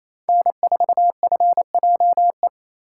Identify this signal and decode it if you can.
Morse code, heard as a keyed tone, about 32 wpm, 700 Hz. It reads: N4FJE
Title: Speed Racing - USA 1x3 Call Signs 28wpm